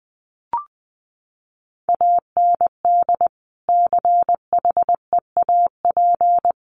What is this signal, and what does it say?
Morse code, heard as a keyed tone, about 20 wpm, 700 Hz. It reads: AND CHEAP
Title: Sentences from Top 1200 Words 20wpm